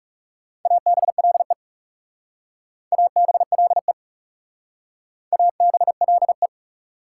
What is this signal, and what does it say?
Morse code, heard as a keyed tone, about 39 wpm, 700 Hz. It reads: ABLE ABLE ABLE